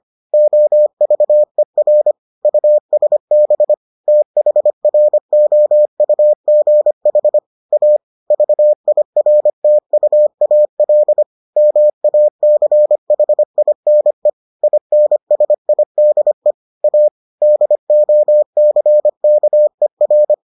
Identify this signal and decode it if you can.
Morse code, heard as a keyed tone, about 25 wpm, 600 Hz. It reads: OVER USB THROUGH A VIRTUAL MACHINE INSIDE A DOCKER